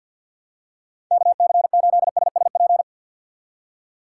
Morse code, heard as a keyed tone, about 52 wpm, 700 Hz. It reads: KC8SSP
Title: USA Call Signs - 6 Characters - 3x Character Spacing - 50wpm